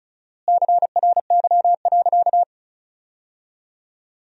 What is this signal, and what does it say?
Morse code, heard as a keyed tone, about 34 wpm, 700 Hz. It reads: CRY.
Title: 4x Word-Spacing - Sentences from Top 500 Words - 35wpm